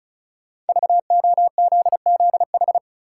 Morse code, heard as a keyed tone, about 34 wpm, 700 Hz. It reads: VOZZH
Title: Code Groups - 5 Letter:Numbers - 35wpm